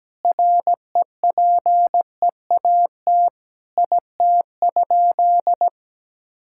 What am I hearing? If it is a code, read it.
Morse code, heard as a keyed tone, about 17 wpm, 700 Hz. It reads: REPEAT IT?
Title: Sentences from Top 1600 Words 17wpm